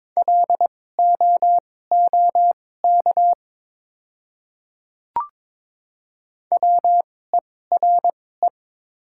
Morse code, heard as a keyed tone, about 22 wpm, spaced as for 16 wpm, 690 Hz. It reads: LOOK WERE
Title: 2x Character Spacing - Top 200 Words - 22wpm